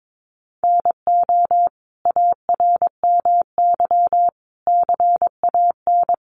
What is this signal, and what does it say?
Morse code, heard as a keyed tone, about 22 wpm, 700 Hz. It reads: NO ARMY CAN
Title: Sentences from Top 1300 Words 22wpm